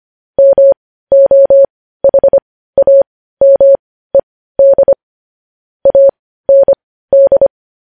Morse code, heard as a keyed tone, about 25 wpm, spaced as for 15 wpm, 550 Hz. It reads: MOHAMED AND
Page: news-headlines-in-morse-code-at-15-wpm